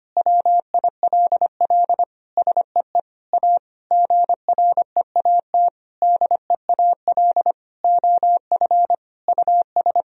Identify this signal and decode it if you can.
Morse code, heard as a keyed tone, about 25 wpm, 700 Hz. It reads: WILL SEE A GREAT DEAL OF US
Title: Sentences from Top 1000 Words - Review Part 4 - 25wpm